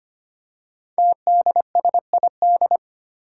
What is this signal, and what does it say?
Morse code, heard as a keyed tone, about 25 wpm, 700 Hz. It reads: TDSID
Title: Five Challenging Letter-Number Combinations 25wpm